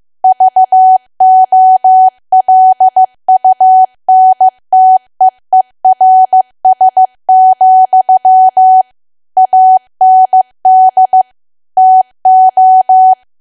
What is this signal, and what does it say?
Morse code, heard as a keyed tone, about 15 wpm, 750 Hz. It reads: VOLUNTEERS, AND TO
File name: cwa / W1AW-Level2 / 15WPM